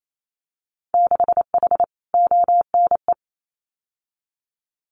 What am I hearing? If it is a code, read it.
Morse code, heard as a keyed tone, about 28 wpm, 700 Hz. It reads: 6H ONE
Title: Winter Field Day Exchanges Encore 28wpm